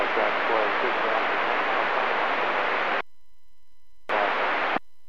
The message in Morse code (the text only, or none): none